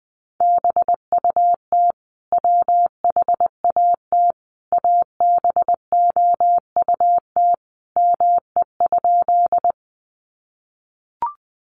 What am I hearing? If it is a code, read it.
Morse code, heard as a keyed tone, about 20 wpm, 700 Hz. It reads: BUT WHAT ABOUT ME?
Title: Sentences from Top 200 Words 20wpm